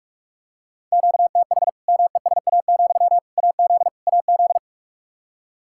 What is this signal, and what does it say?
Morse code, heard as a keyed tone, about 46 wpm, 700 Hz. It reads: QTH MESA, AZ AZ
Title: Practice QSO - Exchange 1 - Beginning - 45wpm